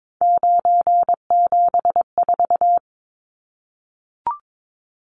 Morse code, heard as a keyed tone, about 22 wpm, 700 Hz. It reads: 974
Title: Numbers - 3-Digit 22wpm